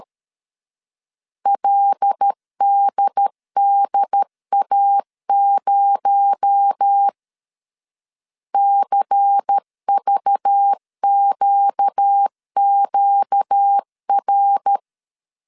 Morse code, heard as a keyed tone, about 13 wpm, 800 Hz. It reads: LDDA0 CVQQR
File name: CW / DET / 1 6 1500C S A 12wpm 3440